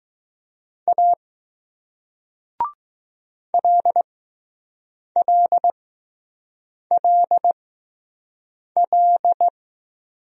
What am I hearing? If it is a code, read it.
Morse code, heard as a keyed tone, about 20 wpm, spaced as for 6 wpm, 700 Hz. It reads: A LLLL